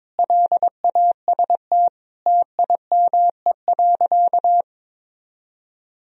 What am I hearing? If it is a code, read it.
Morse code, heard as a keyed tone, about 22 wpm, 700 Hz. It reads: LAST TIME.
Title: Sentences from Top 1000 Words - Review Part 3 - 22wpm